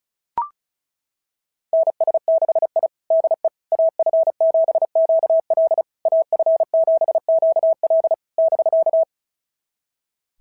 Morse code, heard as a keyed tone, about 36 wpm, 690 Hz. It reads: NS6I DE AF7QL AF7QL <BK>
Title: SOTA QSO Practice 35wpm